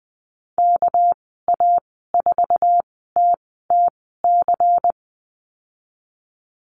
Morse code, heard as a keyed tone, about 20 wpm, spaced as for 15 wpm, 700 Hz. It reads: KA4TTC
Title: 200 Lesson 32g - Letter k - Call Signs 20wpm